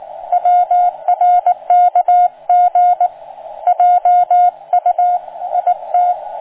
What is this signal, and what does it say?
Morse code, heard as a keyed tone, about 19 wpm, 710 Hz. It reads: WRKG JU IT